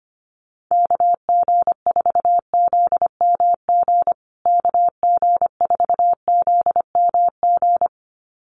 Morse code, heard as a keyed tone, about 25 wpm, 700 Hz. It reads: KG4ZMG KG4ZMG
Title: USA Call Signs Repeated 2x 25wpm